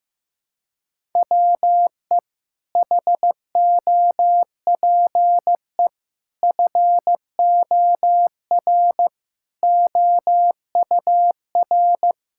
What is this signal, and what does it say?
Morse code, heard as a keyed tone, about 15 wpm, 700 Hz. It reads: WE HOPE FOR OUR
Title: Sentences from Top 1000 Words - Review Part 6 - 15wpm